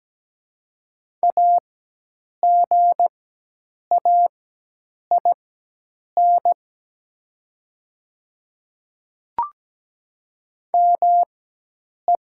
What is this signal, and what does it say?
Morse code, heard as a keyed tone, about 17 wpm, spaced as for 8 wpm, 700 Hz. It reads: AGAIN ME
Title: 4x Character Spacing - Top 200 Words - 17wpm